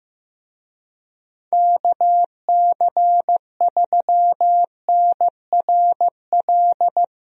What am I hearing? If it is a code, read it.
Morse code, heard as a keyed tone, about 15 wpm, 700 Hz. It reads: KC3NRL
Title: USA Call Signs - 6 Characters - Repeated 3x - 15wpm